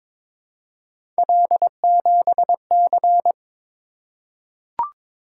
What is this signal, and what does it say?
Morse code, heard as a keyed tone, about 22 wpm, 700 Hz. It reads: L7C
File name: practice / 22wpm / Three Letter-Number Combinations - 22wpm